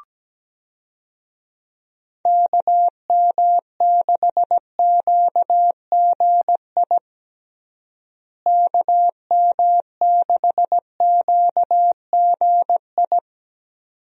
Morse code, heard as a keyed tone, about 17 wpm, 700 Hz. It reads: KM6QGI KM6QGI
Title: USA Call Signs - 6 Characters - Repeated 2x - 17wpm